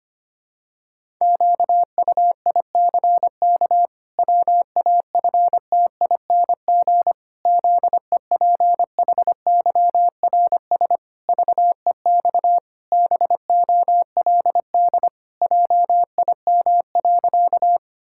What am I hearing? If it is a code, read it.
Morse code, heard as a keyed tone, about 25 wpm, 700 Hz. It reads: QUICK WAFTING ZEPHYRS VEX BOLD JIM.
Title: Pangram Challenge v2 - 25wpm